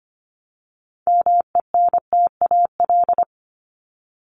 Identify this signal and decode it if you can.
Morse code, heard as a keyed tone, about 25 wpm, 700 Hz. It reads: MENTAL
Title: Top 501-750 Words 25wpm